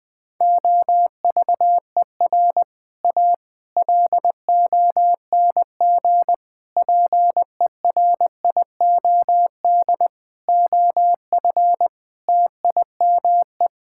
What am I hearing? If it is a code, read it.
Morse code, heard as a keyed tone, about 19 wpm, 700 Hz. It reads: OVER A LONG PERIOD OF TIME
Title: Sentences with 6-Word Phrases - Spoken Beginning - 20wpm